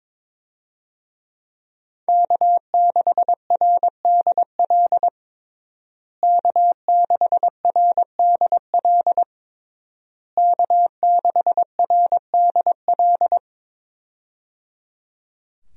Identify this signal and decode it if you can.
Morse code, heard as a keyed tone, about 22 wpm, 700 Hz. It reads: K6RDL K6RDL K6RDL
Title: USA Call Signs - 5 Characters - Repeated 3x - 22wpm